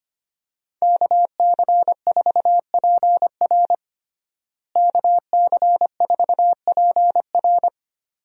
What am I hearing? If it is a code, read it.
Morse code, heard as a keyed tone, about 25 wpm, 700 Hz. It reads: KC4PR KC4PR
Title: USA Call Signs - 5 Characters - Repeated 2x - 25wpm